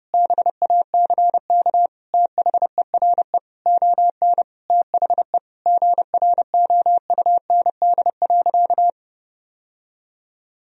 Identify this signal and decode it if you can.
Morse code, heard as a keyed tone, about 32 wpm, 700 Hz. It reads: BACK THERE ON THE GROUND.
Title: Sentences from Top 1000 Words - Review Part 2 - 30wpm